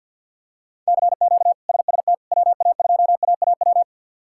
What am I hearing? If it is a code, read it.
Morse code, heard as a keyed tone, about 52 wpm, 700 Hz. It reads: CQ SST WA2AAW
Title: K1USN SST QSO Practice - 50wpm